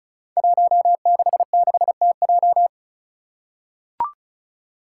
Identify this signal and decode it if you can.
Morse code, heard as a keyed tone, about 36 wpm, 700 Hz. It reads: 166TJ
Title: Five Challenging Letter-Number Combinations 35wpm